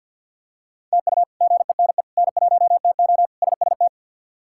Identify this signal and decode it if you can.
Morse code, heard as a keyed tone, about 52 wpm, 700 Hz. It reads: TU GENE N1TO SST
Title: K1USN SST QSO Practice - 50wpm